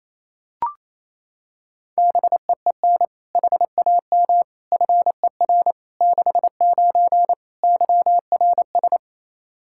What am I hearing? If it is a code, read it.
Morse code, heard as a keyed tone, about 28 wpm, 700 Hz. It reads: BEEN HAM FER 69 YRS
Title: Practice QSO - Exchange 3 - More Info - 28wpm